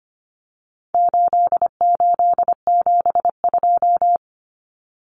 Morse code, heard as a keyed tone, about 25 wpm, 700 Hz. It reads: 8872
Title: Numbers - 4-Digit 25wpm